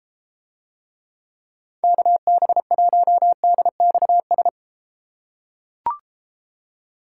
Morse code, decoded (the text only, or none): KB1DXS